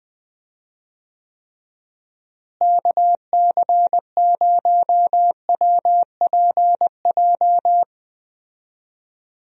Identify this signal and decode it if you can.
Morse code, heard as a keyed tone, about 21 wpm, 700 Hz. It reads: KC0WPJ
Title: USA Call Signs - 6 Characters - Repeated 2x - 20wpm